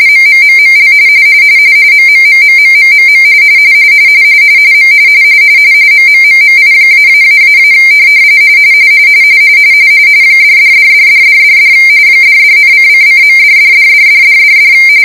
RTTY, about 45.45 baud. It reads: MMMML MMMMMMMM M MMMMI----- $$$$$$$$ $$,:::.$$$$$$$$